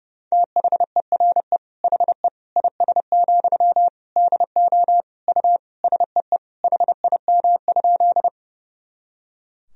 Morse code, heard as a keyed tone, about 30 wpm, 700 Hz. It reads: THERE HE IS, DO U SEE HIM?
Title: Sentences from Top 100 Words and Abbreviations 30wpm